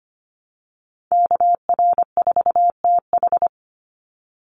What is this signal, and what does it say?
Morse code, heard as a keyed tone, about 25 wpm, 700 Hz. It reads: KR4TH
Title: Call Signs - 25wpm